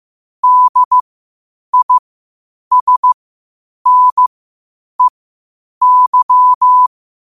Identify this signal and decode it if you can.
Morse code, heard as a keyed tone, about 15 wpm, spaced as for 8 wpm, 1000 Hz. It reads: DISNEY
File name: qotdfiles / 8wpm